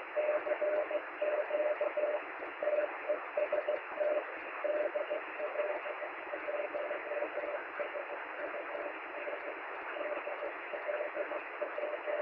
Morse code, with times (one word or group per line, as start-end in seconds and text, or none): none